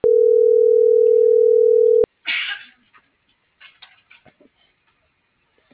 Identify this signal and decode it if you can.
Ringback tone (North America), 440 + 480 Hz.